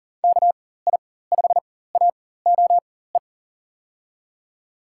Morse code, heard as a keyed tone, about 40 wpm, spaced as for 19 wpm, 700 Hz. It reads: KI5AOE
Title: USA Call Signs - 6 Characters - 4x Character Spacing - 40wpm